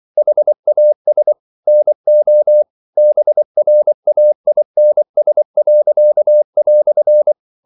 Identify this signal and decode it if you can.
Morse code, heard as a keyed tone, about 24 wpm, 600 Hz. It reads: HAS NO BRAINS."
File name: CW / Practice / EnglishQuotations0224